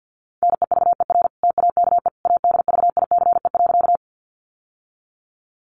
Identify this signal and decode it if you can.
Morse code, heard as a keyed tone, about 50 wpm, 700 Hz. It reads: NEVER TAKE ADVICE.
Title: Sentences from Top 1100 Words 50wpm